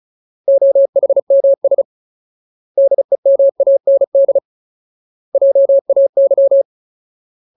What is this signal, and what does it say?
Morse code, heard as a keyed tone, about 34 wpm, 550 Hz. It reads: OHMS DEMAND JAY
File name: Ultimate Word List / 35WPM Ultimate